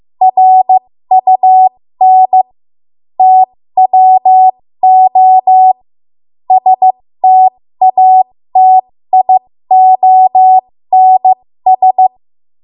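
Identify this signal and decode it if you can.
Morse code, heard as a keyed tone, about 15 wpm, spaced as for 13 wpm, 750 Hz. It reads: RUN TWO STATIONS
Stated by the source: 13 wpm Morse code practice archive transmission.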